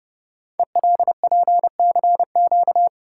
Morse code, heard as a keyed tone, about 30 wpm, 700 Hz. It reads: ELPCQ
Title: Code Groups - 5 Letters - 30wpm